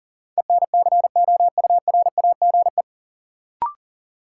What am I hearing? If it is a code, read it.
Morse code, heard as a keyed tone, about 37 wpm, 700 Hz. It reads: ENCOURAGE
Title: Sentences from Top 1400 Words - New Word Drill Alt - 40wpm